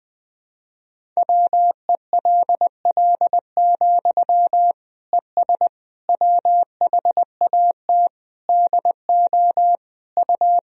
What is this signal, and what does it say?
Morse code, heard as a keyed tone, about 20 wpm, 700 Hz. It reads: WELL, ES WHAT DO U
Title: Sentences from Top 200 Words and Abbreviations 20wpm